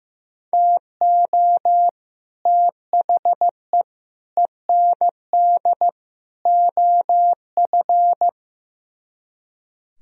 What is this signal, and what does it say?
Morse code, heard as a keyed tone, about 15 wpm, 700 Hz. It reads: TO THE END OF
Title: Speed-Racing - Sets of 4 Words 15wpm